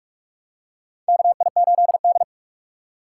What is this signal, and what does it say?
Morse code, heard as a keyed tone, about 41 wpm, 700 Hz. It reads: KI8D
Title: USA Call Signs - 4 Characters - 2x Character Spacing - 45wpm